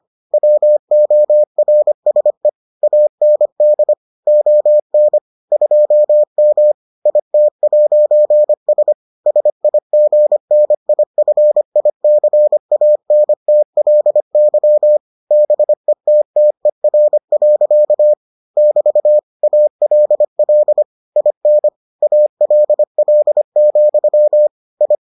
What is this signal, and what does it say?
Morse code, heard as a keyed tone, about 25 wpm, 600 Hz. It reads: WORSE AND ON 2M IT'S SIGNIFICANTLY BETTER. <BT> ALL IN ALL, I